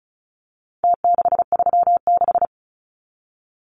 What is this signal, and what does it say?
Morse code, heard as a keyed tone, about 34 wpm, 700 Hz. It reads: T636
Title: Numbers - Contest Style - 4-Digit 35wpm